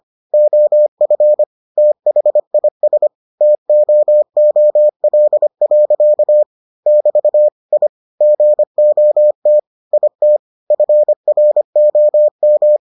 Morse code, heard as a keyed tone, about 25 wpm, 600 Hz. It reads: OF THIS TOOL. <BT> I GOT IT FROM